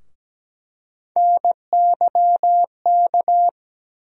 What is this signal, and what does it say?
Morse code, heard as a keyed tone, about 17 wpm, 700 Hz. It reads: NYK
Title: USA Call Sign Postfixes - 3 Letters 17wpm